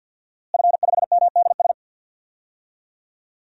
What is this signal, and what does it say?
Morse code, heard as a keyed tone, about 48 wpm, 700 Hz. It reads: F5MDS
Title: DXCC-100 Call Signs - 5 Characters - Repeated 3x - 50wpm